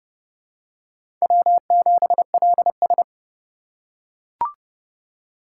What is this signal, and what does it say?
Morse code, heard as a keyed tone, about 31 wpm, 700 Hz. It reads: W7LS
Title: USA Call Signs - 4 Characters - 3x Character Spacing - 30wpm